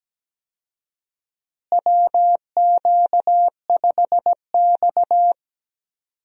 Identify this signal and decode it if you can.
Morse code, heard as a keyed tone, about 17 wpm, 700 Hz. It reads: WQ5X
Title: USA Call Signs - 4 Characters - Repeated 2x - 17wpm